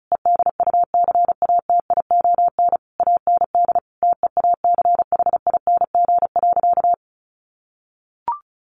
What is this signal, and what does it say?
Morse code, heard as a keyed tone, about 34 wpm, 700 Hz. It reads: EDUCATION AND TEACHING.
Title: Sentences from Top 1100 Words 35wpm